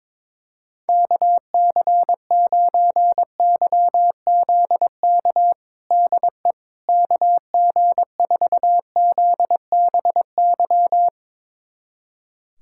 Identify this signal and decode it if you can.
Morse code, heard as a keyed tone, about 22 wpm, 700 Hz. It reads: KC9YZK DE KG4ZBY
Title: USA Callsign DE Callsign 22wpm